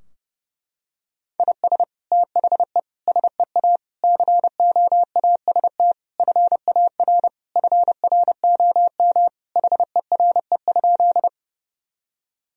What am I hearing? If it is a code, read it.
Morse code, heard as a keyed tone, about 30 wpm, 700 Hz. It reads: IS THE SEA COAST FAR FROM HERE?
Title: Sentences from Top 1000 Words - Review Part 4 - 30wpm